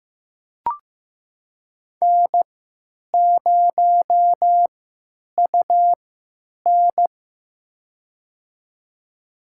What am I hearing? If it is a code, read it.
Morse code, heard as a keyed tone, about 15 wpm, spaced as for 9 wpm, 700 Hz. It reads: N0UN
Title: USA Call Signs - 4 Characters - 3x Character Spacing - 15wpm